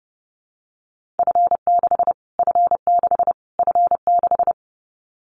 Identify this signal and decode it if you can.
Morse code, heard as a keyed tone, about 30 wpm, 700 Hz. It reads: F6 F6 F6
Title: Callsign Prefixes for Top 25 DXCC - Repeated 3x - 30wpm